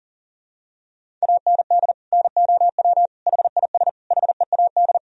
Morse code, heard as a keyed tone, about 40 wpm, 700 Hz. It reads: AND NOW HIS HEAD
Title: Sentences from Top 1000 Words - Review Part 1 - 40wpm